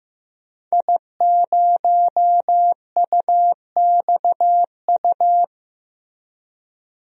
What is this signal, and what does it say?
Morse code, heard as a keyed tone, about 15 wpm, 700 Hz. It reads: I0UXU
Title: DXCC-100 Call Signs - 5 Characters - Repeated 2x - 15wpm